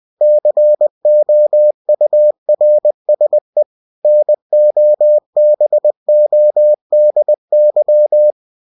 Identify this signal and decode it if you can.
Morse code, heard as a keyed tone, about 20 wpm, 600 Hz. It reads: COURSE NOBODY
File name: cw-books / winnie-the-pooh / winnie-20wpm-page-0018